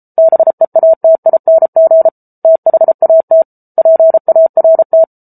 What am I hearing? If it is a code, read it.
Morse code, heard as a keyed tone, about 34 wpm, 650 Hz. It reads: BEATING THAT PART